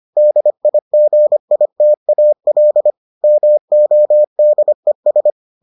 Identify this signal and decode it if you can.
Morse code, heard as a keyed tone, about 25 wpm, 600 Hz. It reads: DIGITAL MODES